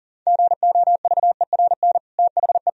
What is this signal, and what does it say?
Morse code, heard as a keyed tone, about 40 wpm, 700 Hz. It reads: GOVERN THE
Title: Sentences from Top 1000 Words - Review Part 2 - 40wpm